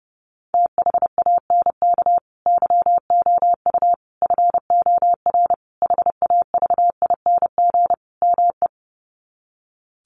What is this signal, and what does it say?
Morse code, heard as a keyed tone, about 30 wpm, 700 Hz. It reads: THANK YOU FOR HAVING ME
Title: Sets of 5 Words Encore 30wpm